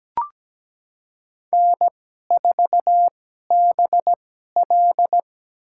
Morse code, heard as a keyed tone, about 17 wpm, spaced as for 12 wpm, 700 Hz. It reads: N4BL
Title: USA Call Signs - 4 Characters - 2x Character Spacing - 17wpm